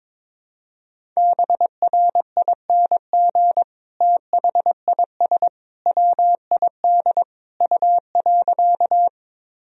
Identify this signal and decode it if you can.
Morse code, heard as a keyed tone, about 21 wpm, 700 Hz. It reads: BRING THIS WID U.
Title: Sentences from Top 500 Words and Abbreviations 22wpm